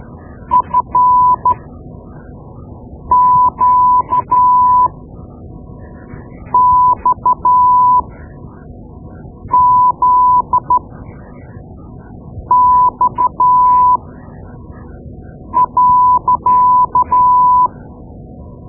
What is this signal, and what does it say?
Morse code, heard as a keyed tone, about 12 wpm, spaced as for 5 wpm, 1000 Hz. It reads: FQXZX.
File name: Morse Lessons / Morse Lessons / Tape-2B